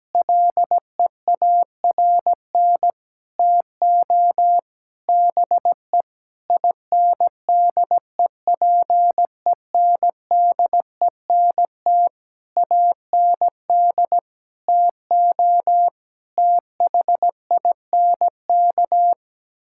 Morse code, heard as a keyed tone, about 17 wpm, 700 Hz. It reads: LEARN TO BE INDEPENDENT AND TO THINK
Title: Sentences from Top 1000 Words - Review Part 6 - 17wpm